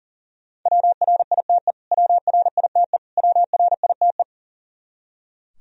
Morse code, heard as a keyed tone, about 40 wpm, 700 Hz. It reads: WRITE WRITE WRITE
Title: Top 100 Words Repeated 40wpm